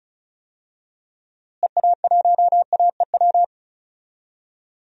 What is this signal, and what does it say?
Morse code, heard as a keyed tone, about 36 wpm, 700 Hz. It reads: EA1AEW
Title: DXCC-100 Call Signs - 6 Characters - Repeated 3x - 35wpm